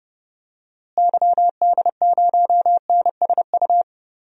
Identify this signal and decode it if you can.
Morse code, heard as a keyed tone, about 32 wpm, 700 Hz. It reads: YD0NSU
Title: DXCC-100 Call Signs - 6 Characters - Repeated 3x - 30wpm